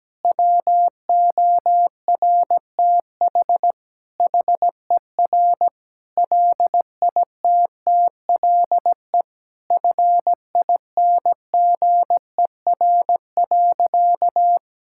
Morse code, heard as a keyed tone, about 17 wpm, 700 Hz. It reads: WORTH HER LITTLE FINGER.
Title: Sentences from Top 1600 Words 17wpm